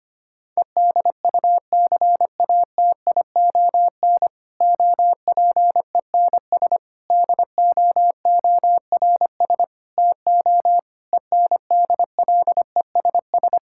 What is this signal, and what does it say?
Morse code, heard as a keyed tone, about 25 wpm, 700 Hz. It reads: EDUCATION OPENS DOORS TO ENDLESS
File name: practice / 25wpm / Difficult Sentences with Hints - 25wpm